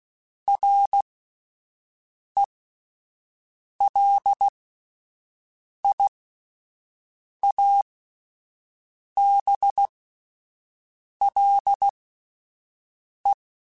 Morse code, heard as a keyed tone, about 16 wpm, spaced as for 5 wpm, 800 Hz. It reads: RELIABLE
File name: QSOs / QSOs-mp3s / WD0FBT 1@10